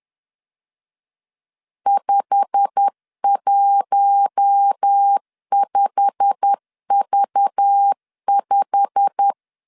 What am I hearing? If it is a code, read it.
Morse code, heard as a keyed tone, about 10 wpm, 800 Hz. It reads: 515V5
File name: LEZIONI / 8 YJ45 / 8 YJ45 10wpm 2549